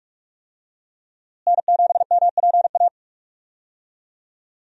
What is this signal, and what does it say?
Morse code, heard as a keyed tone, about 41 wpm, 700 Hz. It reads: N7MPA
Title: USA Call Signs - 5 Characters - Repeated 3x - 45wpm